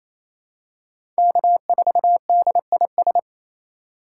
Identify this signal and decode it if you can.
Morse code, heard as a keyed tone, about 27 wpm, 700 Hz. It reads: K4DIS
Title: USA Call Signs - 5 Characters - 2x Character Spacing - 28wpm